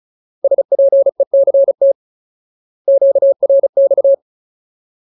Morse code, heard as a keyed tone, about 36 wpm, 550 Hz. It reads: SPECT QRX
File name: Ultimate Word List / 35WPM Ultimate